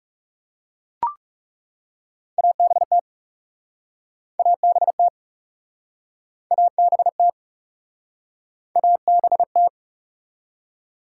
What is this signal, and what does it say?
Morse code, heard as a keyed tone, about 40 wpm, 700 Hz. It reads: ABT ABT ABT ABT